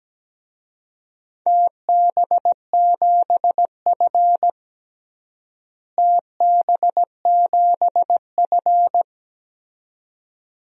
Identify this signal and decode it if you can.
Morse code, heard as a keyed tone, about 17 wpm, 700 Hz. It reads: TB7F TB7F
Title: DXCC-100 Call Signs - 4 Characters - Repeated 2x - 17wpm